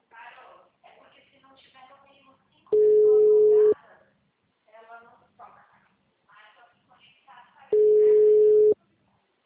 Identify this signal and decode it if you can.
Ringback tone (Europe, 425 Hz), with speech.